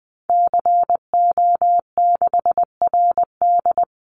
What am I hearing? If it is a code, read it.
Morse code, heard as a keyed tone, about 20 wpm, 700 Hz. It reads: CO6RD
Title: Call Signs Repeated 3x - 20wpm